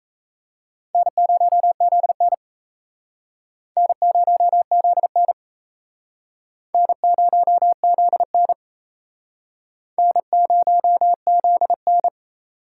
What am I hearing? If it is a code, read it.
Morse code, heard as a keyed tone, about 37 wpm, 700 Hz. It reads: N0ZN N0ZN N0ZN N0ZN